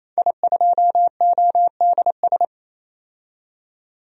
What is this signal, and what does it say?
Morse code, heard as a keyed tone, about 27 wpm, 700 Hz. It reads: I2ODS
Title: DXCC-100 Call Signs - 5 Characters - 28wpm